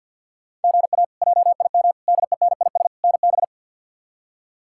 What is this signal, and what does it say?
Morse code, heard as a keyed tone, about 52 wpm, 690 Hz. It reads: GA JIM DENIS NB
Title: K1USN SST QSO Practice - 50wpm